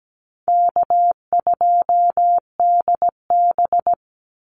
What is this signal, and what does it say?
Morse code, heard as a keyed tone, about 17 wpm, 700 Hz. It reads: K2DB
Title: Easy Call Signs - 17wpm